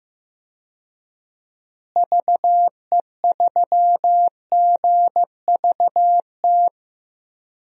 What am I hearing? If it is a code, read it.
Morse code, heard as a keyed tone, about 15 wpm, 700 Hz. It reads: VE3GVT
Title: DXCC-100 Call Signs - 6 Characters - Repeated 3x - 15wpm